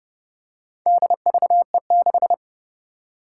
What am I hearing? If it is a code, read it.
Morse code, heard as a keyed tone, about 32 wpm, 700 Hz. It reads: DVE6
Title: Four Challenging Letter-Number Combinations 30wpm